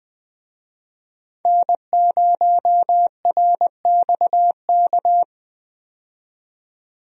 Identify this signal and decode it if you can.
Morse code, heard as a keyed tone, about 20 wpm, 700 Hz. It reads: N0RXK
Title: USA Call Signs - 5 Characters - 20wpm